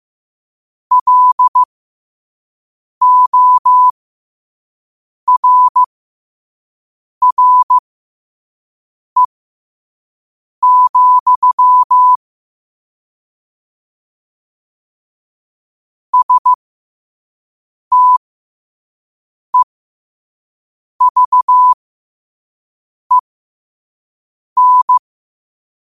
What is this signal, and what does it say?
Morse code, heard as a keyed tone, about 15 wpm, spaced as for 5 wpm, 1000 Hz. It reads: LORRE, STEVEN
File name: 5wpm